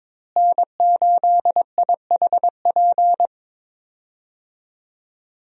Morse code, heard as a keyed tone, about 23 wpm, 700 Hz. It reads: N8IHP
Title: USA Call Signs - 5 Characters - Repeated 3x - 22wpm